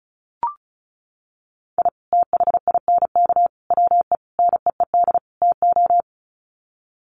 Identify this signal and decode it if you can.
Morse code, heard as a keyed tone, about 34 wpm, 700 Hz. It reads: I THINK WE NEED TO
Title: Sets of 5 Words 35wpm